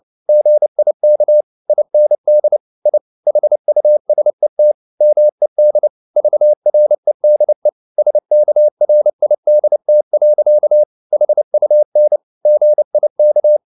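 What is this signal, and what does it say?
Morse code, heard as a keyed tone, about 29 wpm, 600 Hz. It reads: GIK IND I HUSET MED VREDE SKRIDT. HUN GIK
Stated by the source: Fast (QRQ) Morse at 29 wpm, from CW med Gnister.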